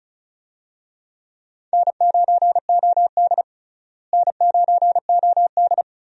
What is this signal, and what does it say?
Morse code, heard as a keyed tone, about 35 wpm, 700 Hz. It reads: N9OD N9OD
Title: USA Call Signs - 4 Characters - Repeated 2x - 35wpm